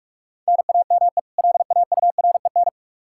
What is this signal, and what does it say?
Morse code, heard as a keyed tone, about 42 wpm, 700 Hz. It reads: NAME LAUREN
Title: K1USN SST QSO Practice 45wpm